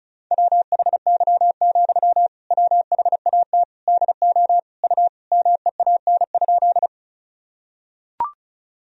Morse code, heard as a keyed tone, about 36 wpm, 700 Hz. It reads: WHY, WHAT DO U MEAN?
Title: Sentences from Top 200 Words and Abbreviations 35wpm